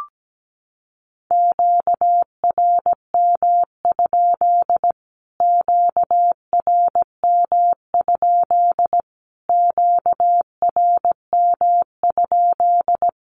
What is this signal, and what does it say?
Morse code, heard as a keyed tone, about 17 wpm, 700 Hz. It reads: QRM? QRM? QRM?
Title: Q-Codes Repeated 3x 17wpm